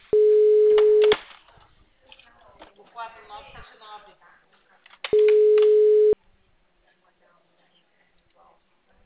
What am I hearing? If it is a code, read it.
Ringback tone (Europe, 425 Hz), with speech.